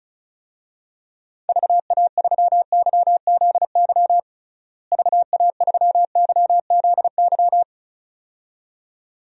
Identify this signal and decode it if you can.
Morse code, heard as a keyed tone, about 34 wpm, 700 Hz. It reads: VA3YZY VA3YZY
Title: DXCC-100 Call Signs - 6 Characters - Repeated 2x - 35wpm